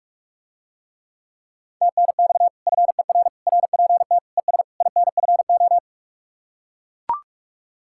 Morse code, heard as a keyed tone, about 44 wpm, 700 Hz. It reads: TNX FER RPT ES INFO